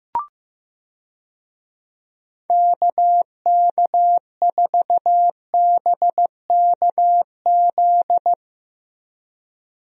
Morse code, heard as a keyed tone, about 15 wpm, 700 Hz. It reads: KK4BKZ
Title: USA Call Signs - 6 Characters - Repeated 3x - 15wpm